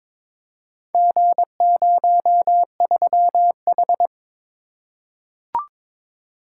Morse code, heard as a keyed tone, about 23 wpm, 700 Hz. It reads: G03H
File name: practice / 22wpm / Four Letter-Number Combinations - 22wpm